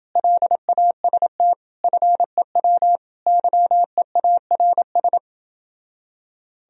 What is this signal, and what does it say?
Morse code, heard as a keyed tone, about 29 wpm, 700 Hz. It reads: LAST FEW YEARS